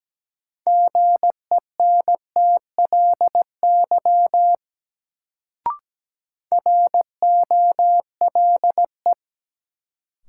Morse code, heard as a keyed tone, about 17 wpm, 700 Hz. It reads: GENTLY ROLE
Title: Sentences from Top 1500 Words - New Word Drill Alt - 17wpm